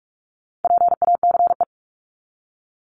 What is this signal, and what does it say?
Morse code, heard as a keyed tone, about 44 wpm, 700 Hz. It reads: PACE
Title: Top 1751-2000 Words 45wpm